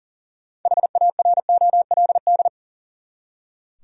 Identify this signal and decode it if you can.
Morse code, heard as a keyed tone, about 37 wpm, 700 Hz. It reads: HAROLD
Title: USA Common Names - 40wpm